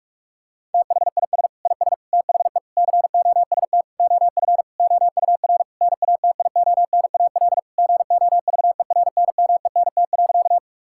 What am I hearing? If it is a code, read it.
Morse code, heard as a keyed tone, about 42 wpm, 700 Hz. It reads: THIS IS THE COST OF OUR NATIONAL GOVERNMENT.
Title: Sentences from Top 1000 Words - Review Part 5 - 45wpm